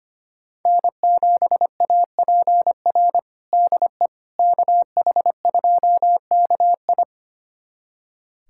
Morse code, heard as a keyed tone, about 25 wpm, 700 Hz. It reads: N7APR DE KH2KI
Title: USA Callsign DE Callsign - 5 Characters - 25wpm